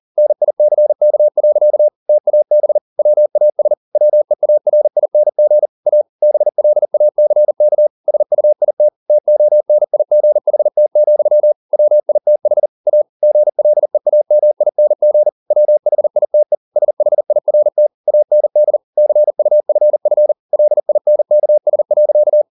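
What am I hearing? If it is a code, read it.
Morse code, heard as a keyed tone, about 40 wpm, 600 Hz. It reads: NICK. TAB WAS WEARING A BLACK SUIT TONIGHT, WITH A GLEAMING WHITE SHIRT AND CUFF LINKS.
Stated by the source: Morse code at 40 wpm, practice text, CW med Gnister.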